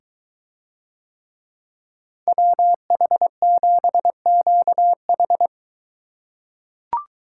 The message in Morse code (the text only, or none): WH7QH